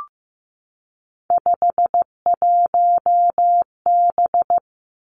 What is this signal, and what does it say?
Morse code, heard as a keyed tone, about 15 wpm, 700 Hz. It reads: 51B
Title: Three Challenging Letter-Number Combinations 15wpm